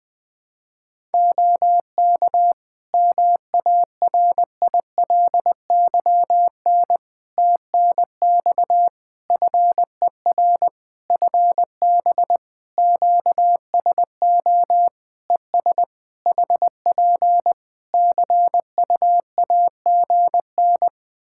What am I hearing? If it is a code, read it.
Morse code, heard as a keyed tone, about 20 wpm, 700 Hz. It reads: OK MARILYN TNX FER FB QSO ES HP CUAGN
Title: Practice QSO - Exchange 4 - Ending - 20wpm